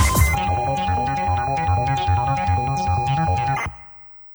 Computer data on cassette.